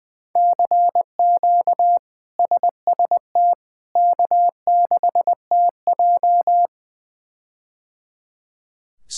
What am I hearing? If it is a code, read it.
Morse code, heard as a keyed tone, about 20 wpm, 700 Hz. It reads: CQ SST K6TJ
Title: K1USN SST QSO Practice - 20wpm